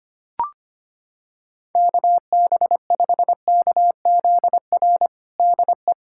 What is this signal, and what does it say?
Morse code, heard as a keyed tone, about 25 wpm, 700 Hz. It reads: KB5KZR DE
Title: USA Callsign DE Callsign - 6 Characters - 25wpm